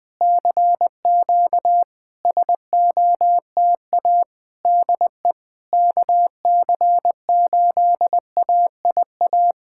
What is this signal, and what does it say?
Morse code, heard as a keyed tone, about 20 wpm, 700 Hz. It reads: CQ SOTA DE KC8AIA
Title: SOTA QSO Practice 20wpm